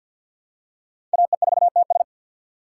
Morse code, heard as a keyed tone, about 49 wpm, 700 Hz. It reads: AE4TS
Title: USA Call Signs - 5 Characters - 6x Character Spacing - 50wpm